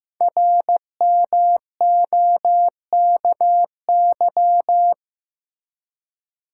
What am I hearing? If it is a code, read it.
Morse code, heard as a keyed tone, about 15 wpm, 700 Hz. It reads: RMOKY
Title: Code Groups - 5 Letters - 15wpm